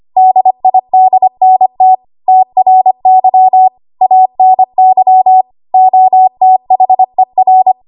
Morse code, heard as a keyed tone, about 25 wpm, 750 Hz. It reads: DIDNT TRY ANY OTHER
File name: Archive / 25 WPM / 25WPM